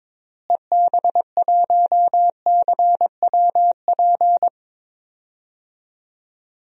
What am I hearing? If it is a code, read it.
Morse code, heard as a keyed tone, about 22 wpm, 700 Hz. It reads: EB1CWP
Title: DXCC-100 Call Signs - 6 Characters - Repeated 3x - 22wpm